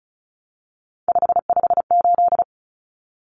Morse code, heard as a keyed tone, about 34 wpm, 700 Hz. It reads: 558
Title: Common Cut Numbers - 3-Digit 35wpm